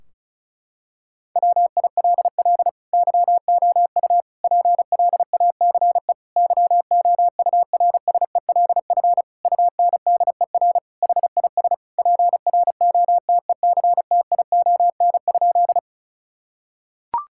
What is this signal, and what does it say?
Morse code, heard as a keyed tone, about 36 wpm, 690 Hz. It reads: WILL YOU PLACE YOURSELF UNDER HIS PROTECTION?
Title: Sentences from Top 1600 Words 35wpm